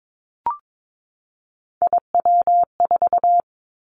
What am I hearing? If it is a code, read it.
Morse code, heard as a keyed tone, about 22 wpm, 700 Hz. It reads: IW4
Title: Callsign Prefixes for Top 25 DXCC - 22wpm